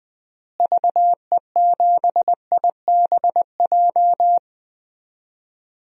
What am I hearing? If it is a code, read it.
Morse code, heard as a keyed tone, about 19 wpm, 700 Hz. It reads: VE7IBJ
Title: DXCC-100 Call Signs - 6 Characters - Repeated 3x - 20wpm